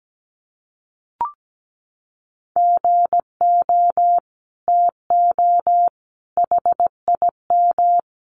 Morse code, heard as a keyed tone, about 17 wpm, 700 Hz. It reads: GO TO HIM
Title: Sentences from Top 1100 Words 17wpm